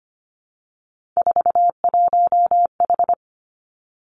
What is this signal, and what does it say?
Morse code, heard as a keyed tone, about 25 wpm, 700 Hz. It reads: 41H
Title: Three Challenging Letter-Number Combinations 25wpm